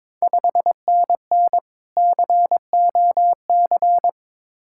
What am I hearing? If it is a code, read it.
Morse code, heard as a keyed tone, about 22 wpm, 700 Hz. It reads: 5NN COC
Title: Ontario QSO Party Exchange 22wpm